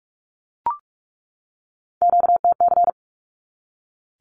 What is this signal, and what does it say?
Morse code, heard as a keyed tone, about 44 wpm, 700 Hz. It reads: QTC
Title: Q-Codes 45wpm